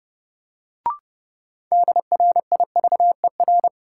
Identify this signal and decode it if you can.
Morse code, heard as a keyed tone, about 32 wpm, 700 Hz. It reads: DRIVER
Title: Sentences from Top 1000 Words - New Word Drill - 30wpm